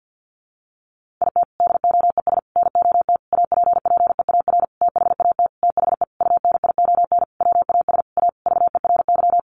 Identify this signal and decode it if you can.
Morse code, heard as a keyed tone, about 50 wpm, 700 Hz. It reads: IT DOES NOT APPEAR THAT THE UNION WAS A VERY
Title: Sentences from Top 1400 Words 50wpm